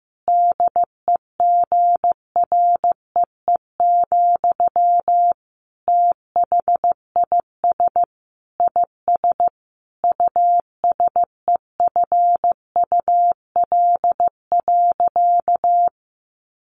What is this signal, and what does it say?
Morse code, heard as a keyed tone, about 15 wpm, 700 Hz. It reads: DEGREE, THIS IS USEFUL.
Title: Sentences from Top 1500 Words 15wpm